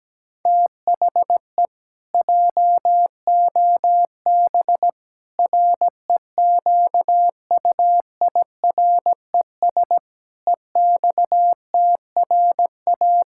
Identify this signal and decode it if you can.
Morse code, heard as a keyed tone, about 17 wpm, 700 Hz. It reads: THE JOB REQUIRES EXTRA
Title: Pangram Challenge v2 - 17wpm